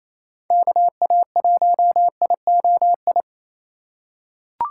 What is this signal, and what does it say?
Morse code, heard as a keyed tone, about 28 wpm, 700 Hz. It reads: KA1IOI
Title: USA Call Signs - 6 Characters - 6x Character Spacing - 28wpm